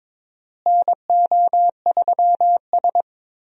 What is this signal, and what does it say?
Morse code, heard as a keyed tone, about 23 wpm, 700 Hz. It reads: NO3S
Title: USA Call Signs - 4 Characters - Repeated 3x - 22wpm